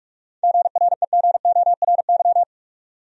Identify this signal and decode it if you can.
Morse code, heard as a keyed tone, about 46 wpm, 700 Hz. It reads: GREGORY
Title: USA Common Names - 45wpm